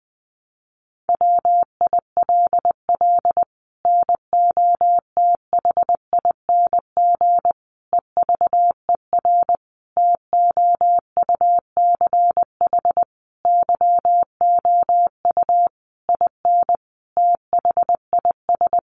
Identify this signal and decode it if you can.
Morse code, heard as a keyed tone, about 20 wpm, 700 Hz. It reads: WILL NOTHING EVER TOUCH YOU IN THIS
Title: Sentences from Top 1200 Words 20wpm